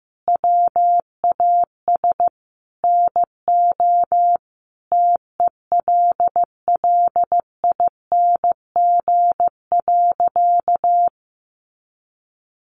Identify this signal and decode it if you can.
Morse code, heard as a keyed tone, about 15 wpm, 700 Hz. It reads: WAS NO TELLING.
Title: A Sound Of Thunder 15wpm